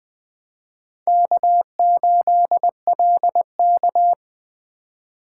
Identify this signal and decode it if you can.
Morse code, heard as a keyed tone, about 20 wpm, 700 Hz. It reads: K8LK
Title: USA Call Signs - 4 Characters - 2x Character Spacing - 20wpm